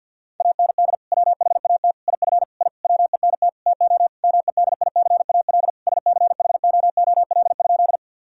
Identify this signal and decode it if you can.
Morse code, heard as a keyed tone, about 51 wpm, 700 Hz. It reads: AND WHAT IF I WENT TO MEDICAL SCHOOL?
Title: Sentences from Top 1000 Words - Review Part 5 - 50wpm